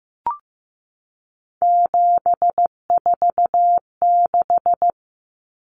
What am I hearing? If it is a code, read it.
Morse code, heard as a keyed tone, about 15 wpm, 700 Hz. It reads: 746
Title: Common Cut Numbers - 3-Digit 15wpm